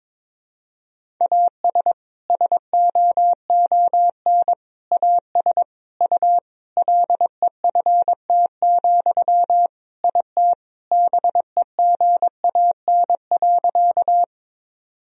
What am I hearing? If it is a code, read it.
Morse code, heard as a keyed tone, about 21 wpm, 700 Hz. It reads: AS SOON AS U LEFT, IT BEGAN.
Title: Sentences from Top 400 Words and Abbreviations 22wpm